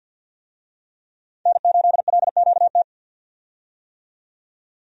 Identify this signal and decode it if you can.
Morse code, heard as a keyed tone, about 50 wpm, 700 Hz. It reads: N8LQT
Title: USA Call Signs - 5 Characters - 50wpm